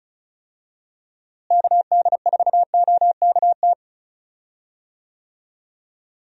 Morse code, heard as a keyed tone, about 34 wpm, 700 Hz. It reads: KD4OKT
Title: USA Call Signs - 6 Characters - Repeated 3x - 35wpm